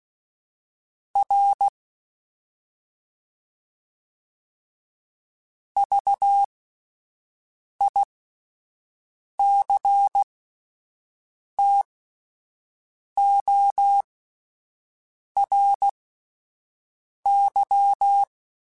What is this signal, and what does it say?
Morse code, heard as a keyed tone, about 16 wpm, spaced as for 5 wpm, 800 Hz. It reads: R VICTORY